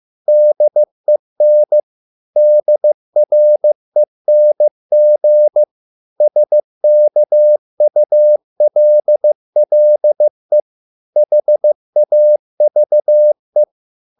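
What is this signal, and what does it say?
Morse code, heard as a keyed tone, about 15 wpm, 600 Hz. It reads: DEN DRENG SKULLE HAVE
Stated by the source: Morse code at a slow 15 wpm from CW med Gnister.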